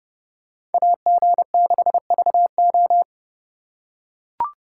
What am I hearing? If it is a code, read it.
Morse code, heard as a keyed tone, about 29 wpm, 700 Hz. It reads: AG6VO